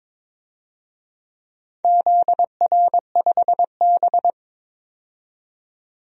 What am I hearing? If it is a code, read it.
Morse code, heard as a keyed tone, about 22 wpm, 700 Hz. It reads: ZR5B
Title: DXCC-100 Call Signs - 4 Characters - 22wpm